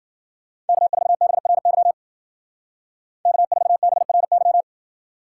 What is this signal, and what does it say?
Morse code, heard as a keyed tone, about 51 wpm, 700 Hz. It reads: K4BRY K4BRY